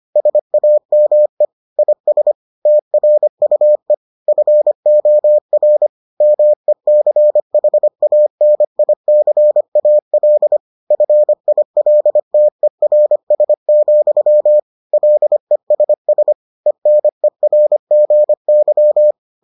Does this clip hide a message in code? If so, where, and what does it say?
Yes: Morse code from 0.2 to 19.1 s, heard as a keyed tone, about 25 wpm, 600 Hz. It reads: SAME IS TRUE FOR MECHANICAL FILTERS, LESS ENERGY